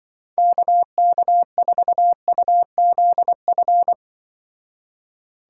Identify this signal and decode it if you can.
Morse code, heard as a keyed tone, about 24 wpm, 700 Hz. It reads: KK4UZF
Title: Speed Racing - USA Call Signs 20wpm